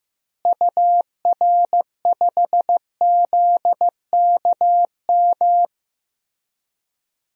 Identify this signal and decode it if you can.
Morse code, heard as a keyed tone, about 15 wpm, 700 Hz. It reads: UR5ZKM
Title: DXCC-100 Call Signs - 6 Characters - Repeated 3x - 15wpm